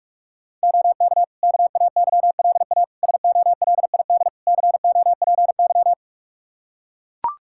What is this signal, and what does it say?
Morse code, heard as a keyed tone, about 41 wpm, 700 Hz. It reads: OK KAYLA SOLID COPY
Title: K1USN SST QSO Practice 45wpm